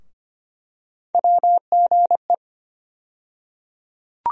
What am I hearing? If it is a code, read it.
Morse code, heard as a keyed tone, about 25 wpm, 700 Hz. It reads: WGE
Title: USA Call Sign Postfixes - 3 Letters 25wpm